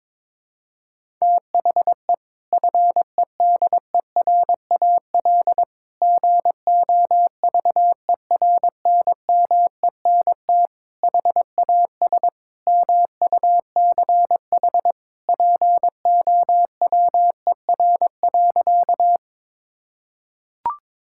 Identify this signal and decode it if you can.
Morse code, heard as a keyed tone, about 22 wpm, 700 Hz. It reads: THE FEDERAL GOVERNMENT HAS MUCH POWER.
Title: Sentences from Top 1000 Words - Review Part 5 - 22wpm